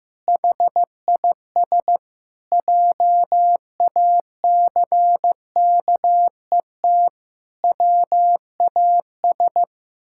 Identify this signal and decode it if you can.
Morse code, heard as a keyed tone, about 15 wpm, 700 Hz. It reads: HIS JACKET WAS
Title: Pangram Challenge v2 - 15wpm